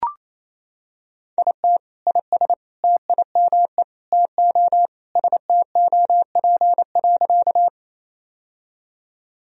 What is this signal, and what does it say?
Morse code, heard as a keyed tone, about 27 wpm, 700 Hz. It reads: IT IS TIME TO STOP.
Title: Sentences from Top 1000 Words - Review Part 1 - 28wpm